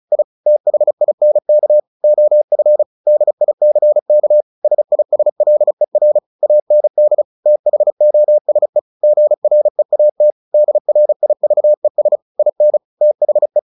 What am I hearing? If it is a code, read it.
Morse code, heard as a keyed tone, about 34 wpm, 600 Hz. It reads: I THINK OF DICK SISLER AND THOSE GREAT DRIVES IN THE
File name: the old man and the sea / the old man and the sea 1 35wpm